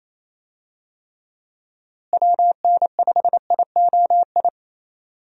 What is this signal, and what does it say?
Morse code, heard as a keyed tone, about 28 wpm, 700 Hz. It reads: WN5IOI
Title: USA Call Signs - 6 Characters - Repeated 2x - 28wpm